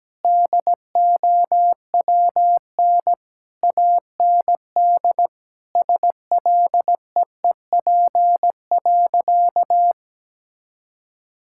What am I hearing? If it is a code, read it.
Morse code, heard as a keyed tone, about 17 wpm, 700 Hz. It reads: DOWN AND SLEEP.
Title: Sentences from Top 1600 Words 17wpm